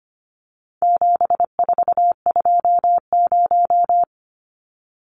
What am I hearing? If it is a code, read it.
Morse code, heard as a keyed tone, about 25 wpm, 700 Hz. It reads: 7420
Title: Numbers - 4-Digit 25wpm